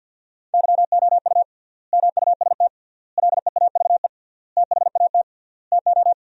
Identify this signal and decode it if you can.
Morse code, heard as a keyed tone, about 50 wpm, 700 Hz. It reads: YOU MUST LEAVE THAT TO
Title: 3x Word-Spacing - Sentences from Top 400 Words - 50wpm